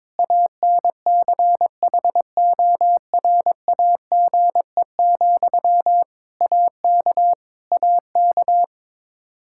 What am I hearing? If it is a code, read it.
Morse code, heard as a keyed tone, about 22 wpm, 700 Hz. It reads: ANCHORAGE, AK AK
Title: Practice QSO - Exchange 1 - Beginning - 22wpm